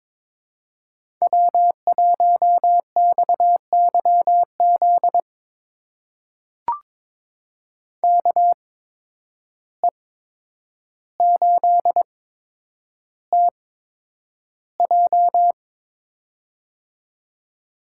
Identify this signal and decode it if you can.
Morse code, heard as a keyed tone, about 22 wpm, spaced as for 9 wpm, 700 Hz. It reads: W1XYZ KE8TJ
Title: USA Call Signs - 5 Characters - 8x Character Spacing - 22wpm